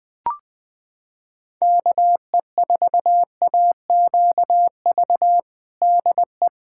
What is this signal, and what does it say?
Morse code, heard as a keyed tone, about 20 wpm, 700 Hz. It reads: KE4AQV DE
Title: USA Callsign DE Callsign 20wpm